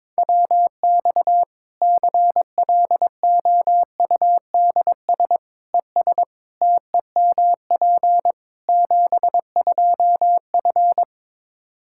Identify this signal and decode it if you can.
Morse code, heard as a keyed tone, about 22 wpm, 700 Hz. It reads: WX CLOUDS ES TEMP 72F
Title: Practice QSO - Exchange 2 - Info - 22wpm